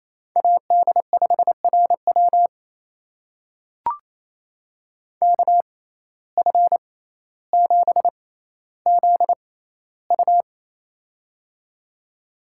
Morse code, heard as a keyed tone, about 28 wpm, spaced as for 14 wpm, 700 Hz. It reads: AD5RW KF7ZU
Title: USA Call Signs - 5 Characters - 6x Character Spacing - 28wpm